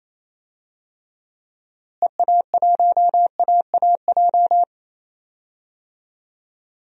Morse code, heard as a keyed tone, about 28 wpm, 700 Hz. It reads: EA1AAJ
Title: DXCC-100 Call Signs - 6 Characters - Repeated 2x - 28wpm